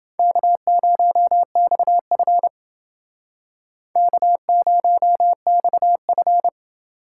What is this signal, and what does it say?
Morse code, heard as a keyed tone, about 29 wpm, 700 Hz. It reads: K0XF K0XF